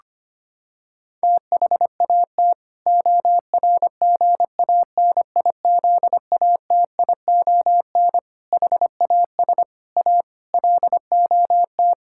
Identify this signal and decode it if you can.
Morse code, heard as a keyed tone, about 25 wpm, 700 Hz. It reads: THAT ORGANIZATION HAS A LOT
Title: Sentences from Top 1000 Words - Review Part 6 - 25wpm